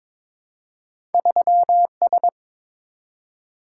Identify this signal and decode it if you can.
Morse code, heard as a keyed tone, about 22 wpm, 700 Hz. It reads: 3S
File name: practice / 22wpm / Two Letter-Number Combinations - 22wpm